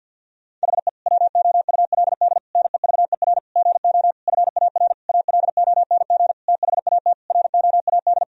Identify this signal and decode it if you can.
Morse code, heard as a keyed tone, about 52 wpm, 700 Hz. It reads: HE WOULD NEVER GO FAR ALONG THAT ROAD
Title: Sentences from Top 1000 Words - Review Part 2 - 50wpm